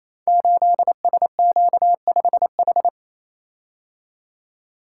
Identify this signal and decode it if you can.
Morse code, heard as a keyed tone, about 29 wpm, 700 Hz. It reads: 8SQ5H
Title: Code Groups - 5 Letter:Numbers - 28wpm